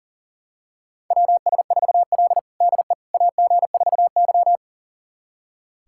Morse code, heard as a keyed tone, about 40 wpm, 700 Hz. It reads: WS4L DE AG4Y
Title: USA Callsign DE Callsign - 4 Characters - 40wpm